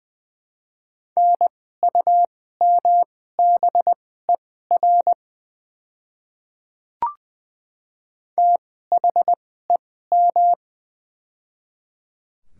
Morse code, heard as a keyed tone, about 20 wpm, spaced as for 14 wpm, 700 Hz. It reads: NUMBER THEM
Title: 2x Character Spacing - Top 100 Words - 20wpm